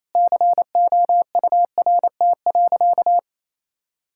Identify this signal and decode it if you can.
Morse code, heard as a keyed tone, about 28 wpm, 700 Hz. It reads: COURT.
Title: Sentences from Top 1600 Words 28wpm